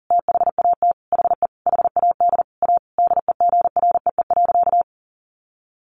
Morse code, heard as a keyed tone, about 40 wpm, 700 Hz. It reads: THAT HE HAD A DEGREE.
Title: Sentences from Top 1500 Words 40wpm